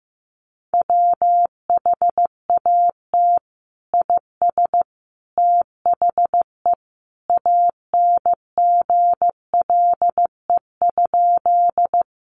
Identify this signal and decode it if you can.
Morse code, heard as a keyed tone, about 15 wpm, 700 Hz. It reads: WHAT IS THE ANGLE?
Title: Sentences from Top 1500 Words 15wpm